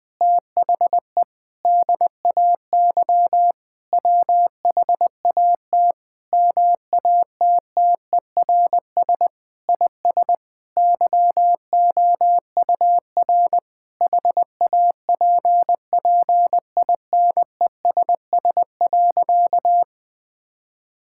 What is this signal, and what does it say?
Morse code, heard as a keyed tone, about 20 wpm, 700 Hz. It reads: THE DAY WHAT MATTERS IS YOUR HAPPINESS.
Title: Sentences with 6-Word Phrases - 20wpm